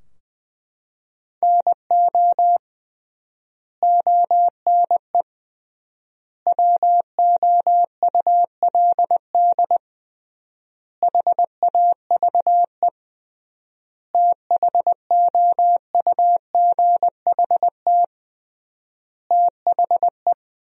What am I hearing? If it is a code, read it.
Morse code, heard as a keyed tone, about 20 wpm, 700 Hz. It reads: NO ONE WOULD HAVE THOUGHT THE
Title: 3x Word-Spacing - Sentences from Top 300 Words - 20wpm